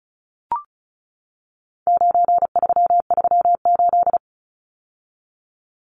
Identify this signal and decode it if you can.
Morse code, heard as a keyed tone, about 34 wpm, 700 Hz. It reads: 9338
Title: Numbers - 4-Digit 35wpm